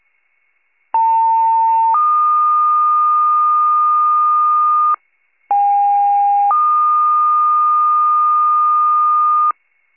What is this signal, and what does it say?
Two-tone page, heard as a one-second tone, then a three-second tone, about 900 then 1220 Hz, about 810 then 1220 Hz. It is heard from 0.9 to 9.5 s.